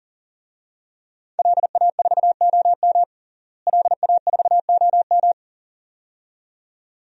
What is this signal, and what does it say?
Morse code, heard as a keyed tone, about 41 wpm, 700 Hz. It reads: LA4OM LA4OM
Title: DXCC-100 Call Signs - 5 Characters - Repeated 2x - 40wpm